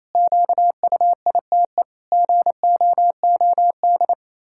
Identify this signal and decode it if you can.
Morse code, heard as a keyed tone, about 28 wpm, 700 Hz. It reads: QUITE GOOD
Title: Pangram Challenge v2 - 28wpm